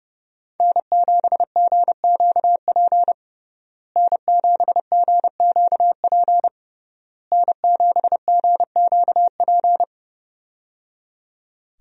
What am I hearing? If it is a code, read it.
Morse code, heard as a keyed tone, about 29 wpm, 700 Hz. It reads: N7GQP N7GQP N7GQP
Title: USA Call Signs - 5 Characters - Repeated 3x - 30wpm